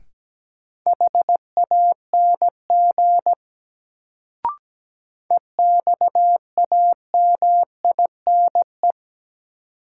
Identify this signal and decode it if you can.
Morse code, heard as a keyed tone, about 17 wpm, 700 Hz. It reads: HANG EXAMINE
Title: Sentences from Top 1600 Words - New Word Drill - 17wpm